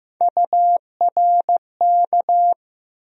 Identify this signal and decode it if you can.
Morse code, heard as a keyed tone, about 15 wpm, 700 Hz. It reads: URK
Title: USA Call Sign Postfixes - 3 Letters 15wpm